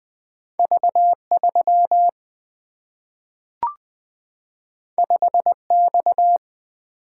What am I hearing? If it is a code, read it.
Morse code, heard as a keyed tone, about 20 wpm, 700 Hz. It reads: V3 5X
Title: Single-Letter Single-Number 20wpm